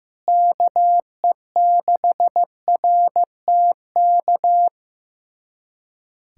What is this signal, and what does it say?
Morse code, heard as a keyed tone, about 15 wpm, 700 Hz. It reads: KE6RTK
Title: USA Callsign DE Callsign - 6 Characters - 15wpm